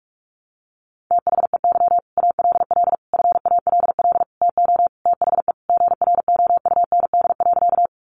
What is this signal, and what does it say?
Morse code, heard as a keyed tone, about 44 wpm, 700 Hz. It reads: THEY ALL FALL TO THE GROUND.
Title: Sentences from Top 500 Words 45wpm